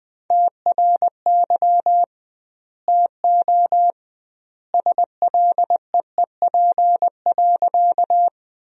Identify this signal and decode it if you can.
Morse code, heard as a keyed tone, about 20 wpm, 700 Hz. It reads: TRY TO SLEEP.
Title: 2x Word-Spacing - Sentences from Top 500 Words - 20wpm